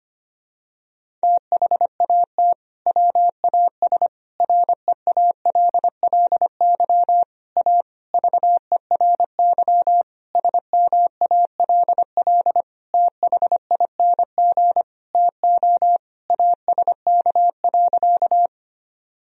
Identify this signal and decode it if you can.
Morse code, heard as a keyed tone, about 25 wpm, 700 Hz. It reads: THAT WAS REALLY A VERY SMALL THING TO ASK.
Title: Sentences from Top 1000 Words - Review Part 1 - 25wpm